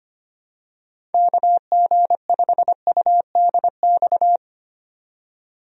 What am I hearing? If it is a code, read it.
Morse code, heard as a keyed tone, about 25 wpm, 700 Hz. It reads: KG5UDX
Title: USA Call Signs - 6 Characters - 25wpm